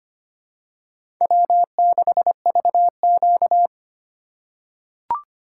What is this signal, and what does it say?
Morse code, heard as a keyed tone, about 25 wpm, 700 Hz. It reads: W6VQ
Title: USA Call Signs - 4 Characters - 3x Character Spacing - 25wpm